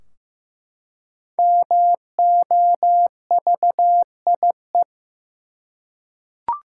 Morse code, heard as a keyed tone, about 15 wpm, 700 Hz. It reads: MOVIE
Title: Five Letter Words - 15wpm